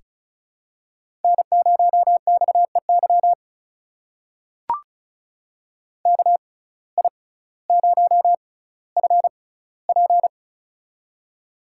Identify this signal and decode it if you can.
Morse code, heard as a keyed tone, about 36 wpm, spaced as for 18 wpm, 700 Hz. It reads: N0XEY KI0FP
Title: USA Call Signs - 5 Characters - 6x Character Spacing - 35wpm